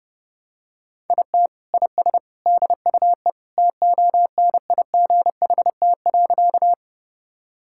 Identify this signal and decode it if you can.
Morse code, heard as a keyed tone, about 30 wpm, 700 Hz. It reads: IT IS DUE TONIGHT.
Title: Sentences from Top 1000 Words - Review Part 6 - 30wpm